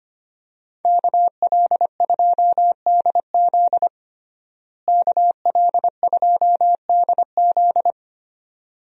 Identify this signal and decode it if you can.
Morse code, heard as a keyed tone, about 25 wpm, 700 Hz. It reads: KL2DZ KL2DZ
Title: USA Call Signs - 5 Characters - Repeated 2x - 25wpm